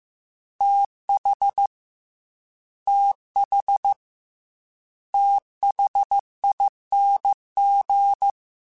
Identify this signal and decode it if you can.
Morse code, heard as a keyed tone, about 15 wpm, 800 Hz. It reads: TH TH THING
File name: Combinations / Combinations-mp3s / Combinations@13